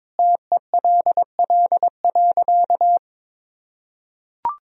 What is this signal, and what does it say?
Morse code, heard as a keyed tone, about 21 wpm, 700 Hz. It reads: TELL.
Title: Sentences from Top 300 Words and Abbreviations 22wpm